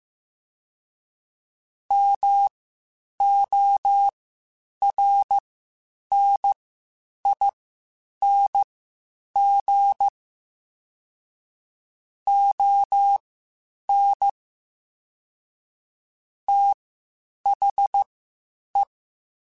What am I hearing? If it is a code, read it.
Morse code, heard as a keyed tone, about 15 wpm, spaced as for 8 wpm, 800 Hz. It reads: MORNING ON THE
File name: Spaced Characters / Spaced Characters-mp3s / Weather 4 24 sp @13